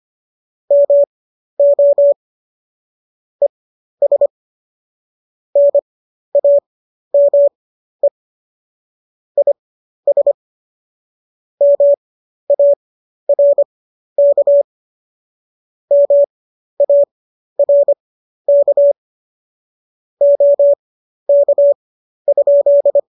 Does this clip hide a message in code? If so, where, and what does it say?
Yes: Morse code from 0.7 to 23.0 s, heard as a keyed tone, about 25 wpm, spaced as for 12 wpm, 570 Hz. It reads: MO ES NAME IS MARK MARK OK?